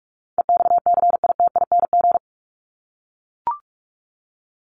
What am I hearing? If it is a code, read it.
Morse code, heard as a keyed tone, about 44 wpm, 700 Hz. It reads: EXCITING
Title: Top 1751-2000 Words 45wpm